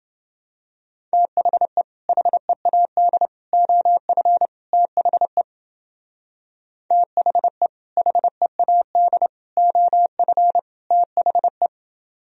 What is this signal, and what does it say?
Morse code, heard as a keyed tone, about 29 wpm, 700 Hz. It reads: THE HEAD OF THE THE HEAD OF THE